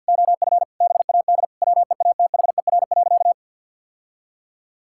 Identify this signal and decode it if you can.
Morse code, heard as a keyed tone, about 46 wpm, 700 Hz. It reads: OF BAD WEATHER.
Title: Sentences from Top 1000 Words - Review Part 6 - 50wpm